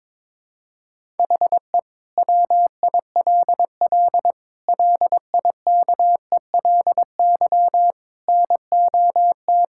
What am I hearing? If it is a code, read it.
Morse code, heard as a keyed tone, about 22 wpm, 700 Hz. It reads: HE WILL LIKELY NOT
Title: Sentences from Top 1000 Words - Review Part 5 - 22wpm